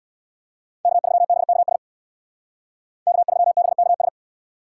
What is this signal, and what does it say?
Morse code, heard as a keyed tone, about 60 wpm, 700 Hz. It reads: K3BKS K3BKS
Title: Speed Racing - USA 1x3 Call Signs 50wpm